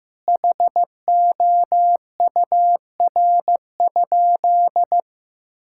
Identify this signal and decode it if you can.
Morse code, heard as a keyed tone, about 15 wpm, 700 Hz. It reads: HOUR?
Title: Sentences from Top 1000 Words - Review Part 5 - 15wpm